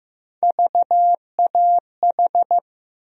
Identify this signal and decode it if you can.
Morse code, heard as a keyed tone, about 15 wpm, 700 Hz. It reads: VAH
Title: USA Call Sign Postfixes - 3 Letters 15wpm